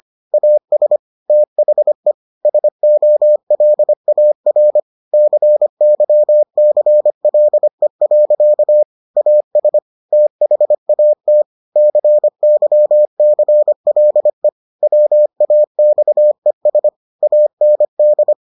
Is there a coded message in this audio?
Yes: Morse code from 0.3 to 18.3 s, heard as a keyed tone, about 25 wpm, 600 Hz. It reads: AS THE SOLAR CYCLE. AS THAT CYCLE WAXES AND